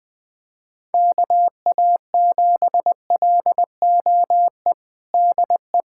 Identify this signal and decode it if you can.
Morse code, heard as a keyed tone, about 20 wpm, 700 Hz. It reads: KA7LOE DE
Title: USA Callsign DE Callsign - 6 Characters - 20wpm